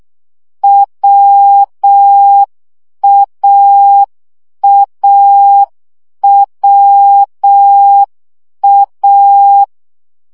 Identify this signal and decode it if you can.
Morse code, heard as a keyed tone, about 6 wpm, 800 Hz. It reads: WAAWA
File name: CW gyakorlas a r w 5 baud